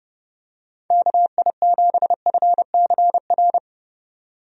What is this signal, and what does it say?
Morse code, heard as a keyed tone, about 30 wpm, 700 Hz. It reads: KI7FCR
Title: USA Call Signs - 6 Characters - Repeated 2x - 30wpm